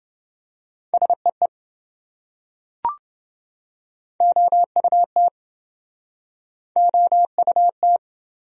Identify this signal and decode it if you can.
Morse code, heard as a keyed tone, about 25 wpm, 700 Hz. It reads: SEE OUT OUT
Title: Speed-Racing Top 100 Words 20wpm